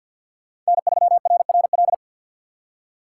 Morse code, heard as a keyed tone, about 58 wpm, 700 Hz. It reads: N3RRL
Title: Speed Racing - USA 1x3 Call Signs 50wpm